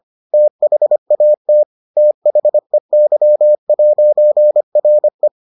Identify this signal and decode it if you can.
Morse code, heard as a keyed tone, about 25 wpm, 600 Hz. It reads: THAT THEY'RE